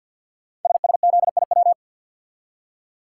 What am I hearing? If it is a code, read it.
Morse code, heard as a keyed tone, about 51 wpm, 700 Hz. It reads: SSZIW
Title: Code Groups - 5 Letters - 50wpm